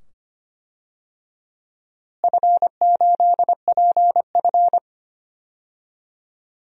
Morse code, heard as a keyed tone, about 25 wpm, 700 Hz. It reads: F8PF
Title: DXCC-100 Call Signs - 4 Characters - Repeated 2x - 25wpm